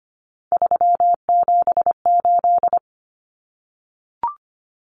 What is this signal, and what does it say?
Morse code, heard as a keyed tone, about 25 wpm, 700 Hz. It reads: 378
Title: Common Cut Numbers - 3-Digit 25wpm